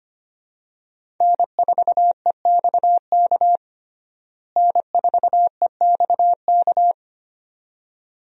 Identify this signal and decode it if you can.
Morse code, heard as a keyed tone, about 25 wpm, 700 Hz. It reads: N4EXK N4EXK
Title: USA Call Signs - 5 Characters - Repeated 2x - 25wpm